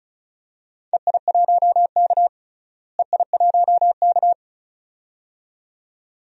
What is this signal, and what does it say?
Morse code, heard as a keyed tone, about 36 wpm, 700 Hz. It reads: EI1K EI1K
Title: DXCC-100 Call Signs - 4 Characters - Repeated 2x - 35wpm